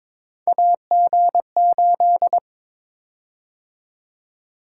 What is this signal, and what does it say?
Morse code, heard as a keyed tone, about 22 wpm, 700 Hz. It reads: AG8
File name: practice / 22wpm / Three Letter-Number Combinations - 22wpm